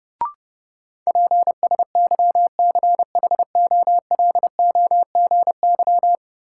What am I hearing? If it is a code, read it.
Morse code, heard as a keyed tone, about 30 wpm, 700 Hz. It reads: PSYCHOLOGY
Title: Sentences from Top 1400 Words - New Word Drill Alt - 30wpm